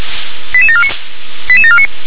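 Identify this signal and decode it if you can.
Five-tone selective call (ZVEI), heard as quick runs of five tones, tones of about 70 ms. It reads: 88749 88749
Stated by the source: Five-tone ZVEI selective call received on an SDR.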